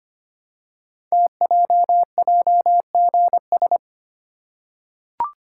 Morse code, heard as a keyed tone, about 25 wpm, 700 Hz. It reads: TJJGS
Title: Code Groups - 5 Letters - 25wpm